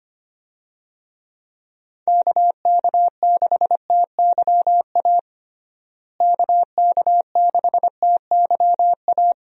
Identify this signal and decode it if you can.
Morse code, heard as a keyed tone, about 25 wpm, 700 Hz. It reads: KK6TYA KK6TYA
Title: USA Call Signs - 6 Characters - Repeated 2x - 25wpm